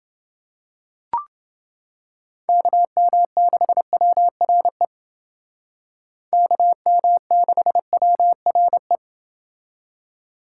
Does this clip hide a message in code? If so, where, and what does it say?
Yes: Morse code from 2.5 to 9.0 s, heard as a keyed tone, about 29 wpm, 700 Hz. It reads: KM6WRE KM6WRE